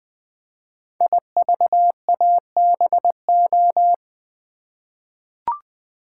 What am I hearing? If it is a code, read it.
Morse code, heard as a keyed tone, about 20 wpm, 700 Hz. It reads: IVABO
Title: Code Groups - 5 Letters - 20wpm